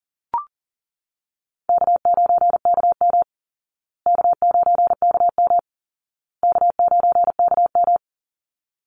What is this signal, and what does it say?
Morse code, heard as a keyed tone, about 37 wpm, 700 Hz. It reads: K9KM K9KM K9KM
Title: Easy Call Signs Repeated 3x - 40wpm